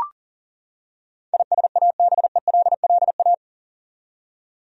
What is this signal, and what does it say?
Morse code, heard as a keyed tone, about 38 wpm, 700 Hz. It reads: ISABELLA
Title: USA Common Names - 40wpm